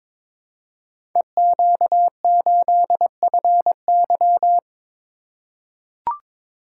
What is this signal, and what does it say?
Morse code, heard as a keyed tone, about 22 wpm, 690 Hz. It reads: EQ8FY
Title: Code Groups - 5 Letter:Numbers - 22wpm